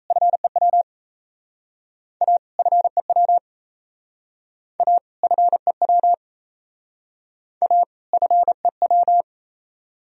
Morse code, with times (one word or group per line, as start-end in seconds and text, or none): 0.1-0.8 FEW
2.2-2.4 A
2.6-3.4 FEW
4.8-5.0 A
5.2-6.1 FEW
7.6-7.8 A
8.1-9.2 FEW